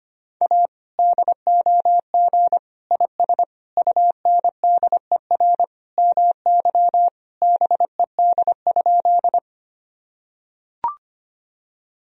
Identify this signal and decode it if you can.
Morse code, heard as a keyed tone, about 25 wpm, 700 Hz. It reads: A DOG IS UNDER MY BED?
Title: Sentences from Top 1000 Words - Review Part 2 - 25wpm